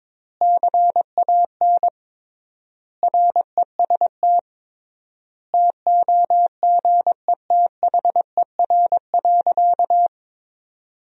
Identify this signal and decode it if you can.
Morse code, heard as a keyed tone, about 21 wpm, 700 Hz. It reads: CAN REST TOGETHER.
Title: 3x Word-Spacing - Sentences from Top 500 Words - 22wpm